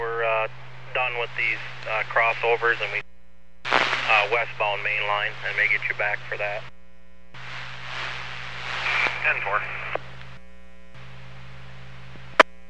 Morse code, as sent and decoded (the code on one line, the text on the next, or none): none